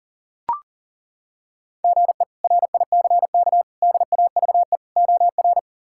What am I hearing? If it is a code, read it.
Morse code, heard as a keyed tone, about 40 wpm, 700 Hz. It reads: GE RICK DAVE OR
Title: K1USN SST QSO Practice - 40wpm